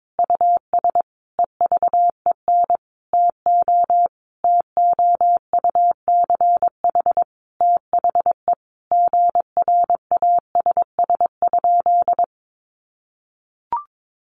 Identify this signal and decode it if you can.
Morse code, heard as a keyed tone, about 22 wpm, 700 Hz. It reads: US EVEN TO TOUCH THE GRASS?
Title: A Sound Of Thunder 22wpm